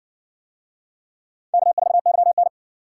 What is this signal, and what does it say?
Morse code, heard as a keyed tone, about 60 wpm, 700 Hz. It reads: K4YN